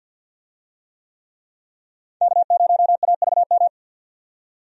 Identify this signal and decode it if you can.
Morse code, heard as a keyed tone, about 51 wpm, 700 Hz. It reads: K0AVM
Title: USA Call Signs - 5 Characters - 50wpm